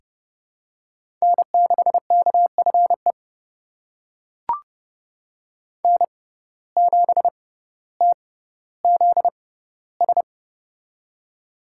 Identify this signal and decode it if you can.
Morse code, heard as a keyed tone, about 30 wpm, spaced as for 15 wpm, 700 Hz. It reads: N6KFE N7TZS
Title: USA Call Signs - 5 Characters - 6x Character Spacing - 30wpm